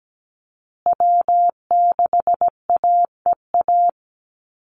Morse code, heard as a keyed tone, about 17 wpm, 700 Hz. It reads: W6AEA
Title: Call Signs Repeated 2x - 17wpm